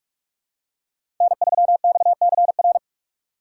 Morse code, heard as a keyed tone, about 43 wpm, 700 Hz. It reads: N3XCR
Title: USA Call Signs - 5 Characters - 4x Character Spacing - 45wpm